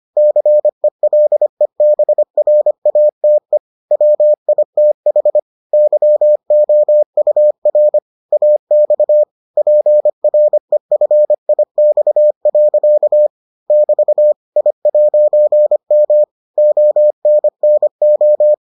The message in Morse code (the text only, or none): CELEBRATE WITH YOUR AX PREFIX. <BT> I'M ONNO